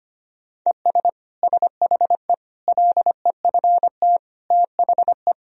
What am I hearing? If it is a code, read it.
Morse code, heard as a keyed tone, about 25 wpm, 700 Hz. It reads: ES SHE LEFT THE
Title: Sentences from Top 300 Words and Abbreviations 25wpm